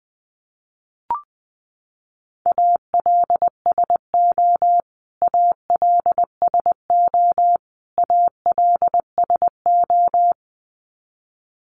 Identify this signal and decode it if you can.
Morse code, heard as a keyed tone, about 19 wpm, 700 Hz. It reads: ALSO ALSO ALSO
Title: Top 500 Words Repeated 20wpm